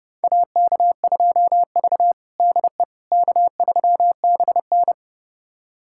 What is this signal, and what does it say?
Morse code, heard as a keyed tone, about 30 wpm, 700 Hz. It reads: AK2V DE K3BN
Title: USA Callsign DE Callsign - 4 Characters - 30wpm